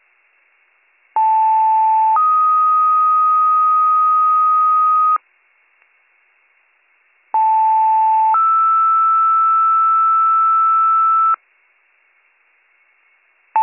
Two-tone page, heard as a one-second tone, then a three-second tone, about 880 then 1250 Hz, about 880 then 1320 Hz. It is heard from 1.2 to 11.3 s.